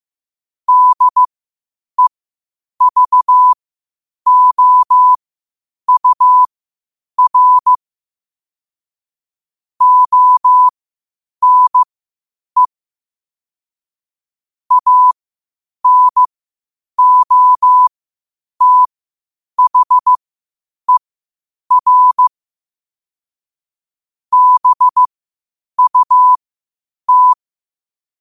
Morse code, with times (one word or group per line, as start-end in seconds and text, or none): 0.7-7.8 DEVOUR
9.8-12.7 ONE
14.7-22.3 ANOTHER
24.3-27.3 BUT